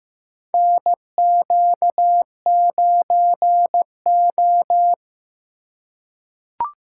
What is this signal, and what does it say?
Morse code, heard as a keyed tone, about 15 wpm, 700 Hz. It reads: NQ9O
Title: USA Call Signs - 4x Character Spacing - 15wpm